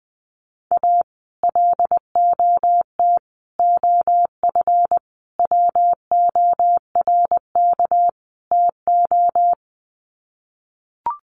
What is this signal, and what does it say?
Morse code, heard as a keyed tone, about 20 wpm, 700 Hz. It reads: A LOT OF WORK TO
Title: Sets of 5 Words Encore2 20wpm